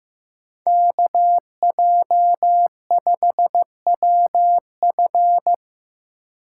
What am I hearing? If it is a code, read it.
Morse code, heard as a keyed tone, about 17 wpm, 700 Hz. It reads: KJ5WF
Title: Speed Racing - USA 2x2 Call Signs 15wpm